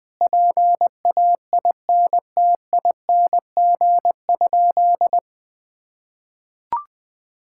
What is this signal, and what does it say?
Morse code, heard as a keyed tone, about 21 wpm, 700 Hz. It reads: PAINTING?
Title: Sentences from Top 1000 Words - Review Part 6 - 20wpm